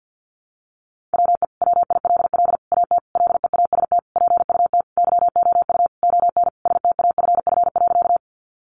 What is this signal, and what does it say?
Morse code, heard as a keyed tone, about 50 wpm, 700 Hz. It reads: WE WILL AT LEAST PUT YOU ON STAFF.
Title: Sentences from Top 1200 Words 50wpm